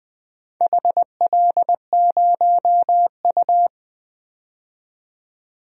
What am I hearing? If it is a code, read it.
Morse code, heard as a keyed tone, about 20 wpm, 700 Hz. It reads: HL0U
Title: DXCC-100 Call Signs - 4 Characters - 20wpm